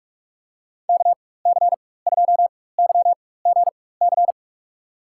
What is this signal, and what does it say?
Morse code, heard as a keyed tone, about 45 wpm, spaced as for 21 wpm, 700 Hz. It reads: KC2YGC
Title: USA Call Signs - 6 Characters - 4x Character Spacing - 45wpm